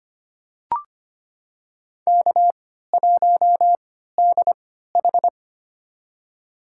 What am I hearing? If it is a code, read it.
Morse code, heard as a keyed tone, about 25 wpm, spaced as for 14 wpm, 700 Hz. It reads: K1DH
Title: USA Call Signs - 4 Characters - 3x Character Spacing - 25wpm